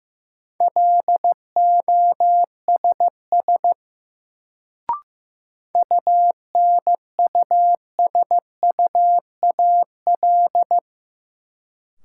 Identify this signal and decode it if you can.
Morse code, heard as a keyed tone, about 15 wpm, 700 Hz. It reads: LOSS UNUSUAL
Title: Sentences from Top 1400 Words - New Word Drill - 15wpm